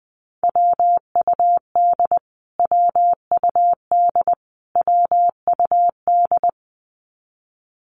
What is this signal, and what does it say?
Morse code, heard as a keyed tone, about 20 wpm, 700 Hz. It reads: WUD WUD WUD
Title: Abbreviations Repeated 3x - 20wpm